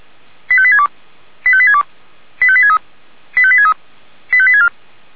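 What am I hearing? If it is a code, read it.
Five-tone selective call (ZVEI), heard as quick runs of five tones, tones of about 70 ms. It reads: 85862 85862 85863 85863 85864
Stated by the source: ZVEI five-tone selective call received on an SDR.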